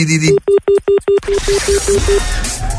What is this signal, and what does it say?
Speech. There is no code.